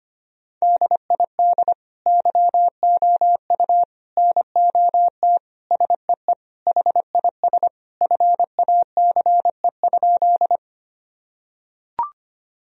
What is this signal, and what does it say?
Morse code, heard as a keyed tone, about 25 wpm, 700 Hz. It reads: DID YOU NOT SEE HIS FACE?
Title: Sentences from Top 1000 Words - Review Part 2 - 25wpm